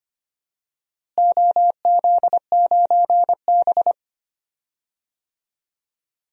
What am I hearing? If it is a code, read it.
Morse code, heard as a keyed tone, about 25 wpm, 700 Hz. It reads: OZ9B
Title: DXCC-100 Call Signs - 4 Characters - Repeated 2x - 25wpm